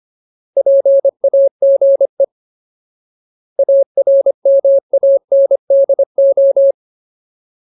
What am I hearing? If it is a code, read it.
Morse code, heard as a keyed tone, about 25 wpm, 550 Hz. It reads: PAGE ARMANDO
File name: Ultimate Word List / 25WPM Ultimate